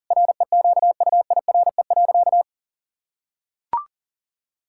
Morse code, heard as a keyed tone, about 40 wpm, 700 Hz. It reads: REQUIRE.
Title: Sentences from Top 1000 Words - Review Part 6 - 40wpm